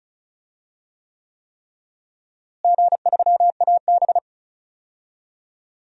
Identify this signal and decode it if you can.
Morse code, heard as a keyed tone, about 36 wpm, 700 Hz. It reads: G3AB
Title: DXCC-100 Call Signs - 4 Characters - Repeated 3x - 35wpm